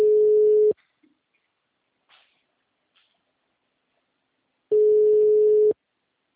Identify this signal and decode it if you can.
Ringback tone (Europe, 425 Hz).
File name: out- -2006- - - .52